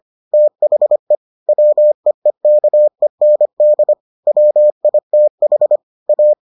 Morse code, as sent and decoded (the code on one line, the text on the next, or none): - .... / .-- . -.- . -. -.. / .-- .. - .... / .-
THE WEEKEND WITH A